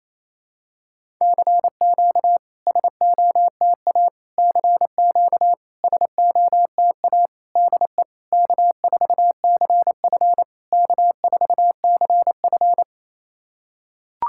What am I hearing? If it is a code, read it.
Morse code, heard as a keyed tone, about 29 wpm, 700 Hz. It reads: CQ SOTA CQ SOTA DE K4CF K4CF
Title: SOTA QSO Practice 28wpm